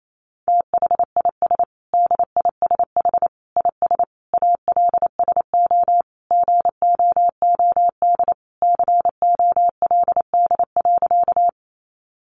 Morse code, heard as a keyed tone, about 28 wpm, 700 Hz. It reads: THIS DISH IS ALSO GOOD COLD.
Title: Sentences from Top 1300 Words 28wpm